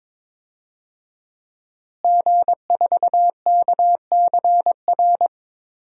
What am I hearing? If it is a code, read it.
Morse code, heard as a keyed tone, about 22 wpm, 700 Hz. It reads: G4KCR
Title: DXCC-100 Call Signs - 5 Characters - Repeated 2x - 22wpm